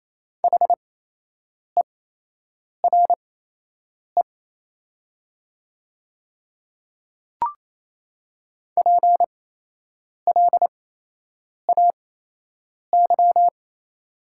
Morse code, heard as a keyed tone, about 28 wpm, spaced as for 8 wpm, 700 Hz. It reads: HERE PLAY
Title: 8x Character Spacing - Top 200 Words - 28wpm